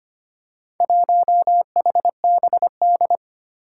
Morse code, heard as a keyed tone, about 25 wpm, 700 Hz. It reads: 1HBD
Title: Four Challenging Letter-Number Combinations 25wpm